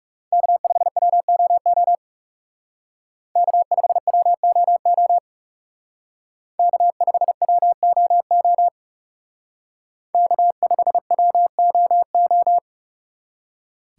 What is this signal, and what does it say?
Morse code, heard as a keyed tone, about 38 wpm, 700 Hz. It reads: K5WOO K5WOO K5WOO K5WOO